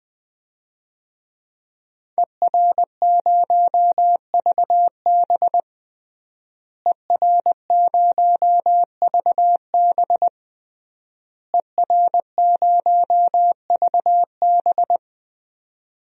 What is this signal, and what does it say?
Morse code, heard as a keyed tone, about 20 wpm, 700 Hz. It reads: ER0VB ER0VB ER0VB
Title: DXCC-100 Call Signs - 5 Characters - Repeated 3x - 20wpm